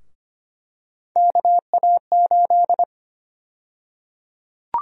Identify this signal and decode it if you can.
Morse code, heard as a keyed tone, about 25 wpm, 700 Hz. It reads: KA8
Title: USA Call Sign Prefixes - 25wpm